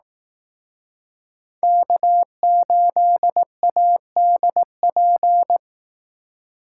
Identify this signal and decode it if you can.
Morse code, heard as a keyed tone, about 18 wpm, 700 Hz. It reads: K8ADP
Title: Speed Racing - USA 1x3 Call Signs 15wpm